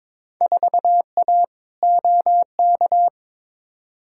Morse code, heard as a keyed tone, about 22 wpm, 700 Hz. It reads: 4A OK
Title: Field Day Exchanges - 22wpm